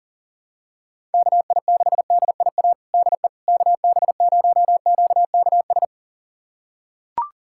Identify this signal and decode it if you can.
Morse code, heard as a keyed tone, about 40 wpm, 700 Hz. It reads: KI6DIA DE KB0QKS
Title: USA Callsign DE Callsign 40wpm